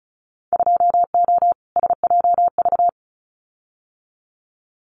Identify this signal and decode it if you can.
Morse code, heard as a keyed tone, about 34 wpm, 700 Hz. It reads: 2O SJV
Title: Winter Field Day Exchanges Encore 35wpm